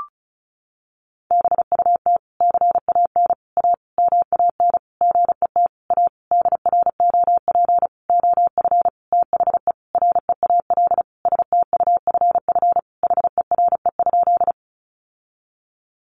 Morse code, heard as a keyed tone, about 34 wpm, 700 Hz. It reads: BUT CAN A MAN GET A DROP OF THE REAL STUFF HERE?
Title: Sentences from Top 1400 Words 35wpm